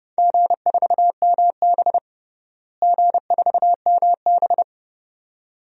Morse code, heard as a keyed tone, about 30 wpm, 700 Hz. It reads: G4MB G4MB
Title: DXCC-100 Call Signs - 4 Characters - Repeated 2x - 30wpm